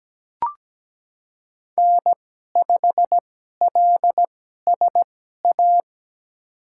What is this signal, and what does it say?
Morse code, heard as a keyed tone, about 17 wpm, spaced as for 12 wpm, 700 Hz. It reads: N5LSA
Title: USA Call Signs - 5 Characters - 2x Character Spacing - 17wpm